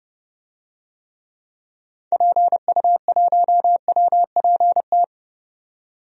Morse code, heard as a keyed tone, about 30 wpm, 700 Hz. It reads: PU1WPT
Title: DXCC-100 Call Signs - 6 Characters - 30wpm